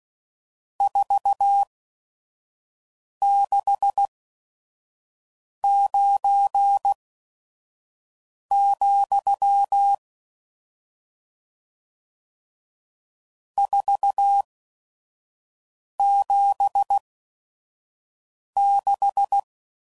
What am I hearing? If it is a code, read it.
Morse code, heard as a keyed tone, about 16 wpm, spaced as for 5 wpm, 800 Hz. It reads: 469, 476
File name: CW Code Groups @5WPM / RST's / Various RST No's